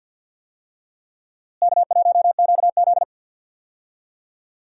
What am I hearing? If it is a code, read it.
Morse code, heard as a keyed tone, about 46 wpm, 700 Hz. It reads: K1QZ
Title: USA Call Signs - 4 Characters - Repeated 2x - 50wpm